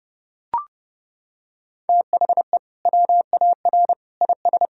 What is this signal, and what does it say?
Morse code, heard as a keyed tone, about 30 wpm, 700 Hz. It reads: THE WAR IS
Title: Sentences from Top 1000 Words - Review Part 4 - 30wpm